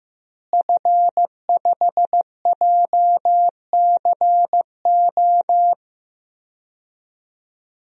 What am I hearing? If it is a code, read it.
Morse code, heard as a keyed tone, about 15 wpm, 700 Hz. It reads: F5JCO
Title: DXCC-100 Call Signs - 5 Characters - Repeated 3x - 15wpm